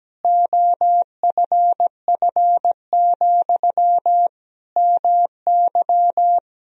Morse code, heard as a keyed tone, about 17 wpm, 700 Hz. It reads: OFF, MY
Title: Sentences from Top 1000 Words - Review Part 2 - 17wpm